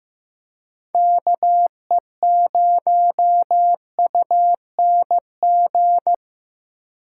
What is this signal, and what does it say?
Morse code, heard as a keyed tone, about 16 wpm, 700 Hz. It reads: KE0UNG
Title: USA Call Signs - 4x Character Spacing - 15wpm